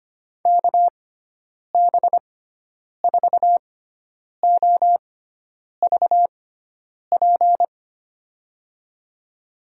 Morse code, heard as a keyed tone, about 25 wpm, spaced as for 9 wpm, 700 Hz. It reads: KB4OVP
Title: USA Call Signs - 6 Characters - 6x Character Spacing - 25wpm